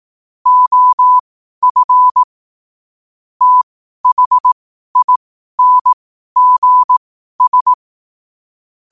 Morse code, heard as a keyed tone, about 18 wpm, spaced as for 12 wpm, 1000 Hz. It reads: OF THINGS
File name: qotdfiles / 12wpm